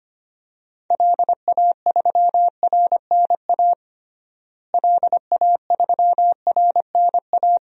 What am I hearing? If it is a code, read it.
Morse code, heard as a keyed tone, about 25 wpm, 700 Hz. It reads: LA3RNA LA3RNA
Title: DXCC-100 Call Signs - 6 Characters - Repeated 2x - 25wpm